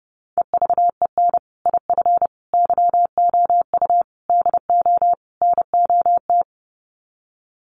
Morse code, heard as a keyed tone, about 30 wpm, 700 Hz. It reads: EVEN IF YOU DO NOT
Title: Sets of 5 Words Encore 30wpm